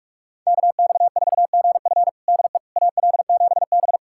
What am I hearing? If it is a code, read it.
Morse code, heard as a keyed tone, about 46 wpm, 700 Hz. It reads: KX4GF DE AL7B
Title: USA Callsign DE Callsign 45wpm